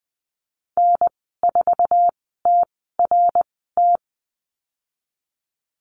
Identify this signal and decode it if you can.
Morse code, heard as a keyed tone, about 20 wpm, spaced as for 15 wpm, 700 Hz. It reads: N4TRT
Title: 055 Lesson 10g - Letter r - Call Signs 20wpm